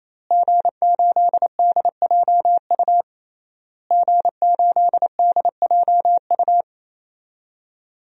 Morse code, heard as a keyed tone, about 28 wpm, 700 Hz. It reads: G8DJU G8DJU
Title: DXCC-100 Call Signs - 5 Characters - Repeated 3x - 28wpm